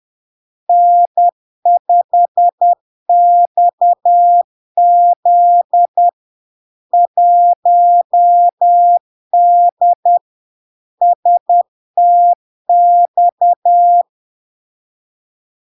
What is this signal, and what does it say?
Morse code, heard as a keyed tone, about 10 wpm, 700 Hz. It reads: N5XZ 1D STX
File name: LARK-FD-2022-Log-in-CW-10wpm-700-Hz